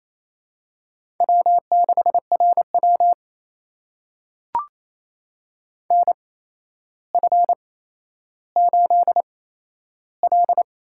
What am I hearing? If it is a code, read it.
Morse code, heard as a keyed tone, about 28 wpm, spaced as for 12 wpm, 700 Hz. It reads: W6RW NF8L
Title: USA Call Signs - 8x Character Spacing - 28wpm